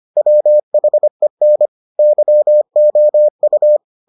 Morse code, heard as a keyed tone, about 25 wpm, 600 Hz. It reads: WHEN YOU